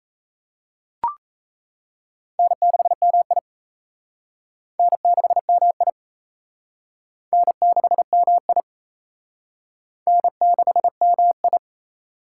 Morse code, heard as a keyed tone, about 34 wpm, 700 Hz. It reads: N6MI N6MI N6MI N6MI
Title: Speed Racing - USA 1x2 Call Signs 28wpm